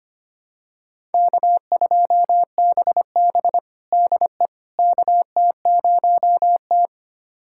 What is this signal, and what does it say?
Morse code, heard as a keyed tone, about 25 wpm, 700 Hz. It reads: K2BB DE KT0T
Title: USA Callsign DE Callsign - 4 Characters - 25wpm